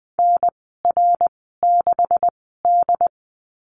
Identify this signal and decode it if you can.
Morse code, heard as a keyed tone, about 20 wpm, spaced as for 15 wpm, 700 Hz. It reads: NR6D
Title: 128 Lesson 21g - Number 6 - Call Signs 20wpm